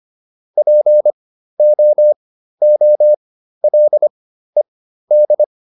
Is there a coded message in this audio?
Yes: Morse code from 0.6 to 5.4 s, heard as a keyed tone, about 25 wpm, spaced as for 13 wpm, 600 Hz. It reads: POOLED